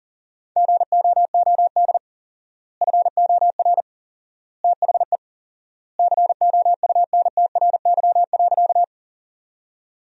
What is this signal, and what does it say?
Morse code, heard as a keyed tone, about 37 wpm, 700 Hz. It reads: GOOD FOR THE COUNTRY.
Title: 4x Word-Spacing - Sentences from Top 300 Words - 40wpm